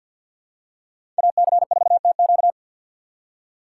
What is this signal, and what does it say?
Morse code, heard as a keyed tone, about 51 wpm, 700 Hz. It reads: AC4TQ
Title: USA Call Signs - 5 Characters - 3x Character Spacing - 50wpm